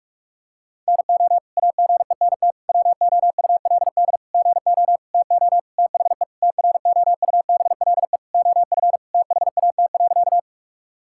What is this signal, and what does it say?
Morse code, heard as a keyed tone, about 46 wpm, 700 Hz. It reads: NO AGENT WOULD GO TO THE TROUBLE OF THAT.
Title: Sentences from Top 1600 Words 45wpm